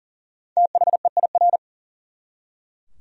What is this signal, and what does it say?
Morse code, heard as a keyed tone, about 40 wpm, 700 Hz. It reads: THEIR
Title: Top 100 Words 40wpm